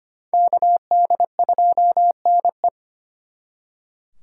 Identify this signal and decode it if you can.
Morse code, heard as a keyed tone, about 31 wpm, 700 Hz. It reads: KD2NE